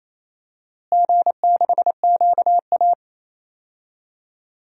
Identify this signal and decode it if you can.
Morse code, heard as a keyed tone, about 28 wpm, 700 Hz. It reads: G6QA
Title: DXCC-100 Call Signs - 4 Characters - Repeated 3x - 28wpm